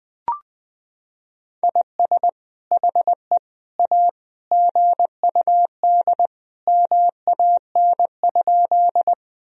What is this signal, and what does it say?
Morse code, heard as a keyed tone, about 19 wpm, 700 Hz. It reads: IS HE A GUD MAN?
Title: Sentences from Top 200 Words and Abbreviations 20wpm